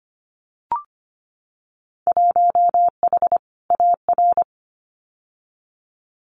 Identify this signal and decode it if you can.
Morse code, heard as a keyed tone, about 25 wpm, 700 Hz. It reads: 1H AR
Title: Winter Field Day Exchanges Encore 25wpm